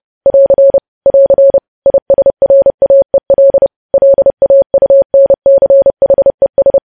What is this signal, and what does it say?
Morse code, heard as a keyed tone, about 30 wpm, 550 Hz. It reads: <AR> <AR> ISRAEL LAUNCHES
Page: news headlines in morse code at 30 wpm